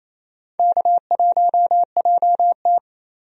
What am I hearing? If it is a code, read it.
Morse code, heard as a keyed tone, about 28 wpm, 700 Hz. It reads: K1JT
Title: USA Call Signs - 4 Characters - 28wpm